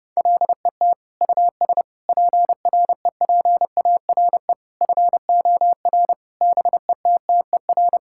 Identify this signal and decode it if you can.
Morse code, heard as a keyed tone, about 29 wpm, 700 Hz. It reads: LET US PREPARE FOR BETTER
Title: Sentences from Top 1000 Words - Review Part 6 - 30wpm